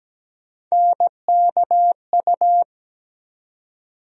Morse code, heard as a keyed tone, about 17 wpm, 700 Hz. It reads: NKU
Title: USA Call Sign Postfixes - 3 Letters 17wpm